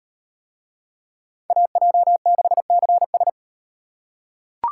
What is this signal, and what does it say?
Morse code, heard as a keyed tone, about 38 wpm, 700 Hz. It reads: AJ6CS